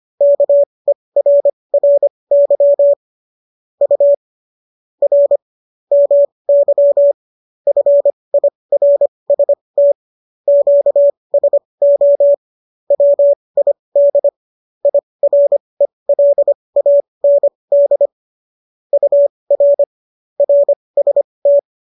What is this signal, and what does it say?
Morse code, heard as a keyed tone, about 25 wpm, spaced as for 20 wpm, 570 Hz. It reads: KERRY U R MY FIRST QSO WID IRELAND UR RST